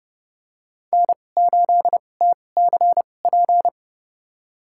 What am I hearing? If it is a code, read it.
Morse code, heard as a keyed tone, about 30 wpm, spaced as for 22 wpm, 700 Hz. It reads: N8TCP
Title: USA Call Signs - 5 Characters - 2x Character Spacing - 30wpm